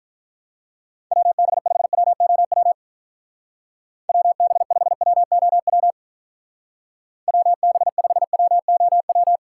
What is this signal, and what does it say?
Morse code, heard as a keyed tone, about 44 wpm, 700 Hz. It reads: WB5WOW WB5WOW WB5WOW
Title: Speed Racing - USA 2x3 Call Signs 35wpm